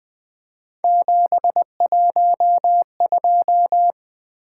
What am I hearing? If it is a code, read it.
Morse code, heard as a keyed tone, about 20 wpm, 700 Hz. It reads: 712
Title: Three Letter-Number Combinations - 20wpm